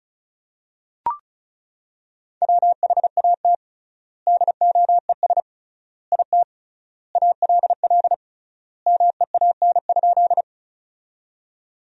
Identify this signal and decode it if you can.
Morse code, heard as a keyed tone, about 36 wpm, 700 Hz. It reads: WHAT DOES IT ALL MEAN?
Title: 3x Word-Spacing - Sentences from Top 200 Words - 35wpm